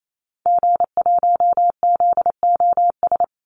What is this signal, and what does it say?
Morse code, heard as a keyed tone, about 28 wpm, 700 Hz. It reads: G1ZOS
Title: Call Signs - 28wpm